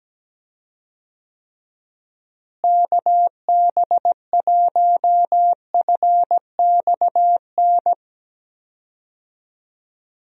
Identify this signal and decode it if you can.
Morse code, heard as a keyed tone, about 17 wpm, 700 Hz. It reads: KB1FXN
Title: USA Call Signs - 6 Characters - Repeated 3x - 17wpm